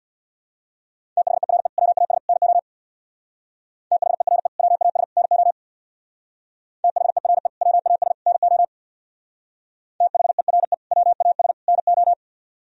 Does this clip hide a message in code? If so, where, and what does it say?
Yes: Morse code from 1.2 to 12.1 s, heard as a keyed tone, about 64 wpm, 700 Hz. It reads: THERE WAS NO THERE WAS NO THERE WAS NO THERE WAS NO